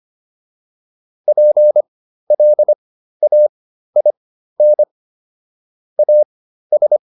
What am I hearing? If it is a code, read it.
Morse code, heard as a keyed tone, about 25 wpm, spaced as for 13 wpm, 600 Hz. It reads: PLAIN AS
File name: PR404 13